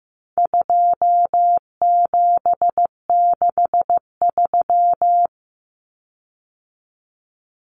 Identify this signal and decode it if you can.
Morse code, heard as a keyed tone, about 15 wpm, 700 Hz. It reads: 2763
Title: Common Cut Numbers - 4-Digit 15wpm